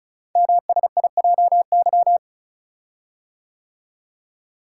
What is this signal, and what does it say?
Morse code, heard as a keyed tone, about 36 wpm, 700 Hz. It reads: MSIJY
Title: Code Groups - 5 Letters - 35wpm